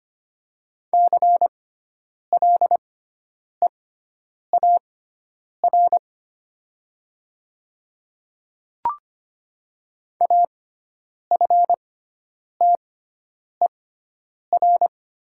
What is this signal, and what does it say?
Morse code, heard as a keyed tone, about 25 wpm, spaced as for 9 wpm, 700 Hz. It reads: CLEAR AFTER
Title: 6x Character Spacing - Top 500 Words - 25wpm